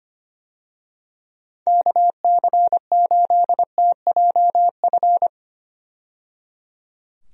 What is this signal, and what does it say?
Morse code, heard as a keyed tone, about 25 wpm, 700 Hz. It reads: KC8TJF
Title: USA Call Signs - 6 Characters - 25wpm